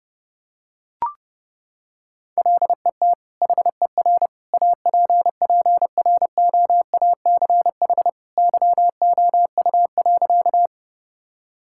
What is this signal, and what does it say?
Morse code, heard as a keyed tone, about 32 wpm, 700 Hz. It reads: LET HER APPROACH YOU.
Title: Sentences from Top 1000 Words - Review Part 6 - 30wpm